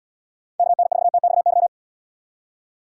Morse code, heard as a keyed tone, about 72 wpm, 700 Hz. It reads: KN4TXY